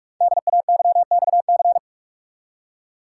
Morse code, heard as a keyed tone, about 44 wpm, 700 Hz. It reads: DAYXC
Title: Code Groups - 5 Letters - 45wpm